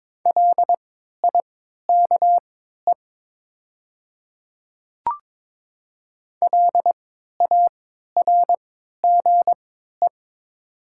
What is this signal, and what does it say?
Morse code, heard as a keyed tone, about 22 wpm, spaced as for 13 wpm, 700 Hz. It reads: LIKE LARGE
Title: 3x Character Spacing - Top 200 Words - 22wpm